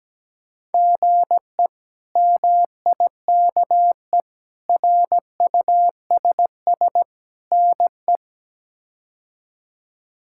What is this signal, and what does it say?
Morse code, heard as a keyed tone, about 17 wpm, 700 Hz. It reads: GE MIKE RUSS NE
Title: K1USN SST QSO Practice - 17wpm